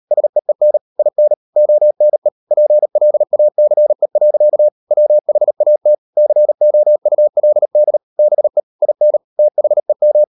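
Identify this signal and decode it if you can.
Morse code, heard as a keyed tone, about 38 wpm, 600 Hz. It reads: SEEN IN ONE PLACE. WHAT COULD BE IN THEM
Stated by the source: Morse code practice at 38 WPM, sending Ken Follett's novel Never in English, from CW med Gnister.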